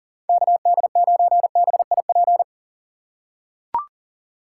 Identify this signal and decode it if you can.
Morse code, heard as a keyed tone, about 40 wpm, 700 Hz. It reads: KD9BIP
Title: USA Call Signs - 6 Characters - 4x Character Spacing - 40wpm